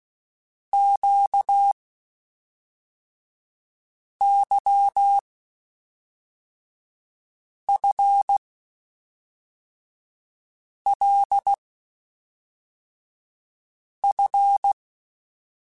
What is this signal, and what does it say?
Morse code, heard as a keyed tone, about 16 wpm, 800 Hz. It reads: Q Y F L F
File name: Basic Sets / Basic Letter Groups Collective7 / LFCYQ@5